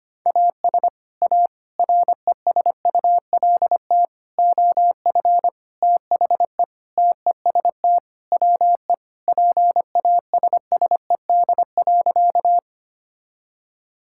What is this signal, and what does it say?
Morse code, heard as a keyed tone, about 25 wpm, 700 Hz. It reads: AS A RESULT OF THE TEST WE PASSED.
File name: practice / 25wpm / Sentences with 4-Word Phrases - 25wpm